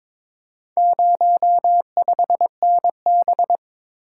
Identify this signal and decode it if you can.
Morse code, heard as a keyed tone, about 22 wpm, 700 Hz. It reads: 05NB
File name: practice / 22wpm / Four Letter-Number Combinations - 22wpm